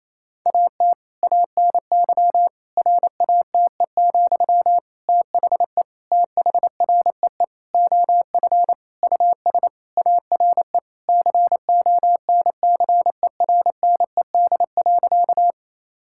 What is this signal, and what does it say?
Morse code, heard as a keyed tone, about 28 wpm, 700 Hz. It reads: AT ANY RATE, THE THREE OF US ARE CONCERNED.
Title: Sentences from Top 1000 Words - Review Part 6 - 28wpm